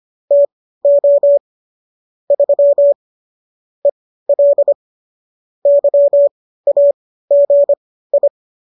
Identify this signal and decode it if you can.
Morse code, heard as a keyed tone, about 25 wpm, spaced as for 15 wpm, 570 Hz. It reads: TO 3 EL YAGI
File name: qso204 15